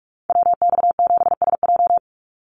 Morse code, heard as a keyed tone, about 44 wpm, 700 Hz. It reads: WX7SJ
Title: Call Signs Repeated 2x - 45wpm